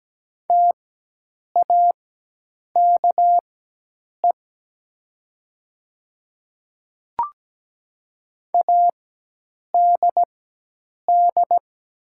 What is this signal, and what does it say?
Morse code, heard as a keyed tone, about 17 wpm, spaced as for 8 wpm, 700 Hz. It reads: TAKE ADD
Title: 4x Character Spacing - Top 200 Words - 17wpm